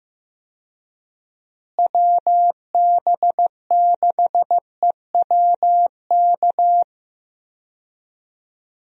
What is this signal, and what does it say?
Morse code, heard as a keyed tone, about 16 wpm, 700 Hz. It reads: WB6EWK
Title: USA Call Signs - 6 Characters - Repeated 3x - 15wpm